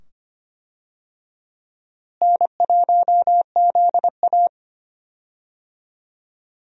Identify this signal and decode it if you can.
Morse code, heard as a keyed tone, about 25 wpm, 700 Hz. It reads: N1ZA
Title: USA Call Signs - 4 Characters - Repeated 2x - 25wpm